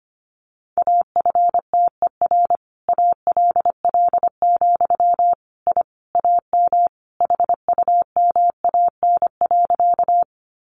Morse code, heard as a keyed tone, about 25 wpm, 700 Hz. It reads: AFTER ALL, I AM HUMAN.
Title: Sentences from Top 600 Words 25wpm